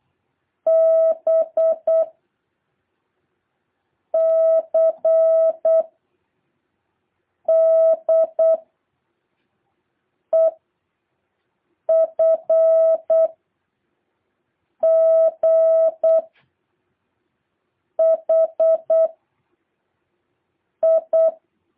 Morse code, heard as a keyed tone, about 8 wpm, 650 Hz. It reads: B C D E F G H I
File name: jh8cbh / 2023cw